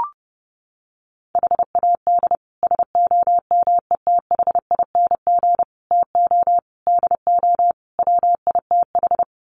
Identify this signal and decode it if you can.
Morse code, heard as a keyed tone, about 30 wpm, 700 Hz. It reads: HAD SOMETHING TO DO WITH
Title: Sets of 5 Words 30wpm